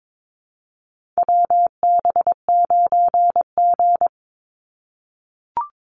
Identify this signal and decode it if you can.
Morse code, heard as a keyed tone, about 22 wpm, 700 Hz. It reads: WB9G
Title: Call Signs - 22wpm